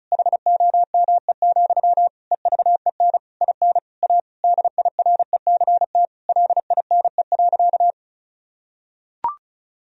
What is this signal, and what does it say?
Morse code, heard as a keyed tone, about 36 wpm, 700 Hz. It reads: HOME, EVEN IN A DIRECT LINE.
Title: Sentences from Top 1000 Words - Review Part 2 - 35wpm